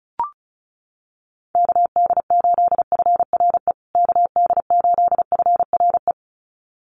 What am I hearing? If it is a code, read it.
Morse code, heard as a keyed tone, about 34 wpm, 700 Hz. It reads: KD8FRE KD8FRE
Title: USA Call Signs Repeated 2x 35wpm